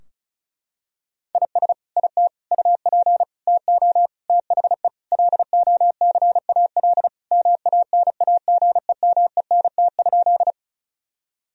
Morse code, heard as a keyed tone, about 36 wpm, 700 Hz. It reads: IS IT UP TO THE LOCAL MANAGEMENT?
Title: Sentences from Top 1000 Words - Review Part 4 - 35wpm